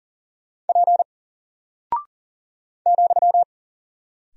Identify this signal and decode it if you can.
Morse code, heard as a keyed tone, about 39 wpm, 700 Hz. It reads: P ,
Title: Single Character (Letter:Number:Punctuation:Prosign) 40wpm